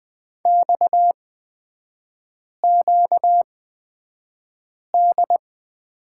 Morse code, heard as a keyed tone, about 20 wpm, spaced as for 5 wpm, 700 Hz. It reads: XQD
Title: 236 Lesson 38e - Letter q - Three Characters 20wpm